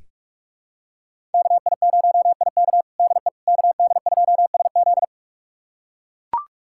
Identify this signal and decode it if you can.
Morse code, heard as a keyed tone, about 44 wpm, 700 Hz. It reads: KI0IK DE KD2SZ
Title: USA Callsign DE Callsign - 5 Characters - 45wpm